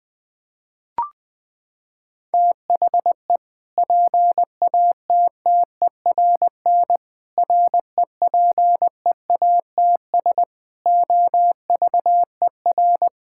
Morse code, heard as a keyed tone, about 20 wpm, 700 Hz. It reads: THE PATTERN REPEATS OVER
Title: Sentences with 6-Word Phrases - 20wpm